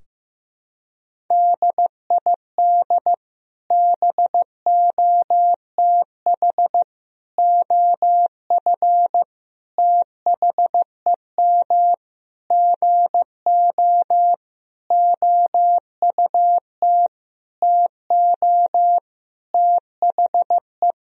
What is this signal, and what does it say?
Morse code, heard as a keyed tone, about 15 wpm, 700 Hz. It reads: DID BOTH OF THEM GO OUT TO THE
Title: Sentences from Top 1000 Words - Review Part 2 - 15wpm